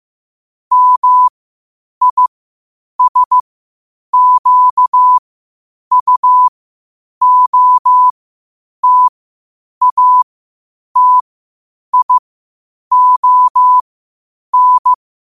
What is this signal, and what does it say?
Morse code, heard as a keyed tone, about 15 wpm, spaced as for 8 wpm, 1000 Hz. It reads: MISQUOTATION
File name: qotdfiles / 8wpm